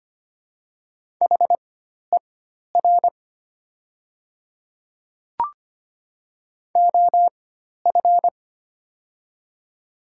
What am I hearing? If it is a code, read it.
Morse code, heard as a keyed tone, about 25 wpm, spaced as for 12 wpm, 700 Hz. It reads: HER OF
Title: 4x Character Spacing - Top 300 Words - 25wpm